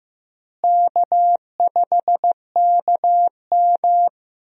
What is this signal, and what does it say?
Morse code, heard as a keyed tone, about 15 wpm, 700 Hz. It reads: K5KM
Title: USA Call Signs - 4 Characters - Repeated 3x - 15wpm